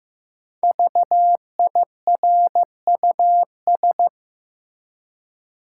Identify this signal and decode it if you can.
Morse code, heard as a keyed tone, about 15 wpm, 700 Hz. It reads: VIRUS
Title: Five Letter Words - 15wpm